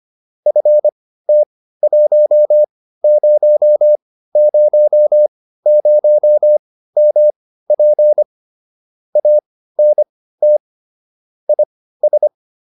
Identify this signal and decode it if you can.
Morse code, heard as a keyed tone, about 25 wpm, spaced as for 15 wpm, 600 Hz. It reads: FT1000MP ANT IS
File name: QSO401 15